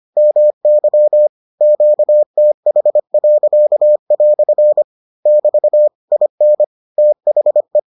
Morse code, heard as a keyed tone, about 25 wpm, 600 Hz. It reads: MY QTH." <BT> IN THE